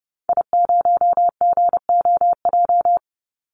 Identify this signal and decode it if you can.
Morse code, heard as a keyed tone, about 30 wpm, 700 Hz. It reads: I0GOJ
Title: Call Signs Repeated 2x - 30wpm